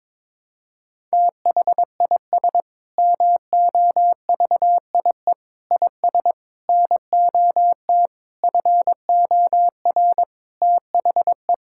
Morse code, heard as a keyed tone, about 22 wpm, 700 Hz. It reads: THIS MOVIE IS NOT FOR THE
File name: Sentences with 7-Word Phrases - 22wpm